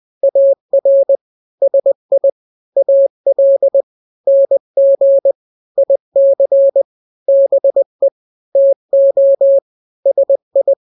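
Morse code, heard as a keyed tone, about 20 wpm, 550 Hz. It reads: AR SI AL NG IC BE TO SI
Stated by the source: Morse drill of two-letter n-grams at 20/18 WPM (Farnsworth), from KK7RR.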